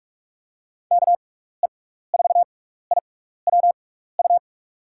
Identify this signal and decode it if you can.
Morse code, heard as a keyed tone, about 42 wpm, spaced as for 15 wpm, 700 Hz. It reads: KE4IWU
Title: USA Call Signs - 6 Characters - 6x Character Spacing - 45wpm